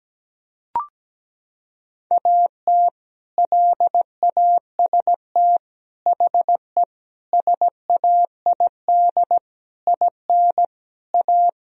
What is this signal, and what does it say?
Morse code, heard as a keyed tone, about 17 wpm, 700 Hz. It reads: AT LAST HE SAID IN A
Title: Sentences from Top 1000 Words - Review Part 1 - 17wpm